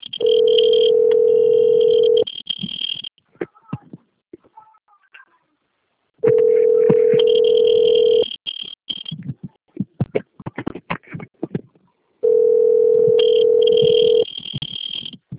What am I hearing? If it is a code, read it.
Ringback tone (North America), 440 + 480 Hz.